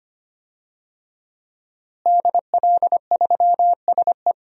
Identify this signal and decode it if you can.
Morse code, heard as a keyed tone, about 25 wpm, 700 Hz. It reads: DL3SE
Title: DXCC-100 Call Signs - 5 Characters - Repeated 2x - 25wpm